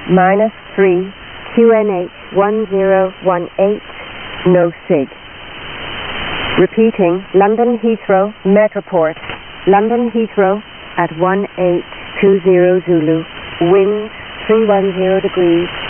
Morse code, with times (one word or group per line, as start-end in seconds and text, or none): none